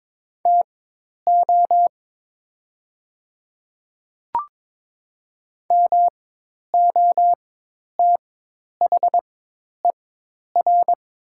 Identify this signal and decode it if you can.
Morse code, heard as a keyed tone, about 22 wpm, spaced as for 10 wpm, 700 Hz. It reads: TO MOTHER
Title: 4x Character Spacing - Top 300 Words - 22wpm